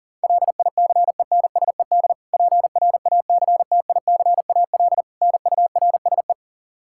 Morse code, heard as a keyed tone, about 37 wpm, 700 Hz. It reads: LICENSED PRACTICAL NURSE
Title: Practice QSO - Exchange 3 - More Info - 40wpm